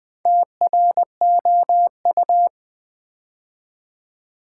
Morse code, heard as a keyed tone, about 20 wpm, 700 Hz. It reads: TROU
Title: Build A Word - 2 Letters At A Time - 8-Letter Words - 20wpm